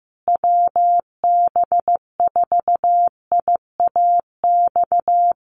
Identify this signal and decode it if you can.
Morse code, heard as a keyed tone, about 15 wpm, 700 Hz. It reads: WB4IAX
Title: USA Call Signs Repeated 3x 15wpm